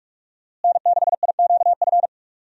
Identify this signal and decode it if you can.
Morse code, heard as a keyed tone, about 45 wpm, 700 Hz. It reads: N6IQF
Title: USA Call Signs - 5 Characters - Repeated 3x - 45wpm